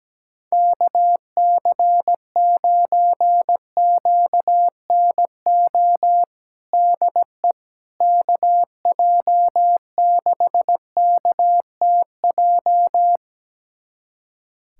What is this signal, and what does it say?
Morse code, heard as a keyed tone, about 17 wpm, 700 Hz. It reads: KC9QNO DE KJ6KTJ
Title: USA Callsign DE Callsign 17wpm